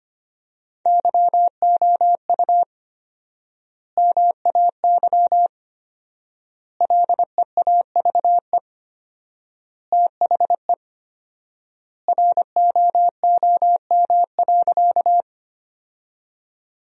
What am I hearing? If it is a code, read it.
Morse code, heard as a keyed tone, about 25 wpm, 700 Hz. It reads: YOU MAY LEAVE THE ROOM.
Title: 4x Word-Spacing - Sentences from Top 400 Words - 25wpm